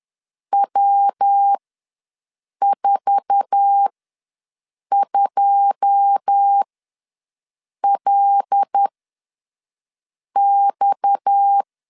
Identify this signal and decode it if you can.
Morse code, heard as a keyed tone, about 10 wpm, spaced as for 5 wpm, 800 Hz. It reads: W42LX
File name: LEZIONI / 9 2 ws 10wpm 4130